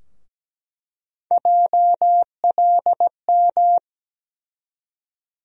Morse code, heard as a keyed tone, about 17 wpm, 700 Hz. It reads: JLM
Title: USA Call Sign Postfixes - 3 Letters 17wpm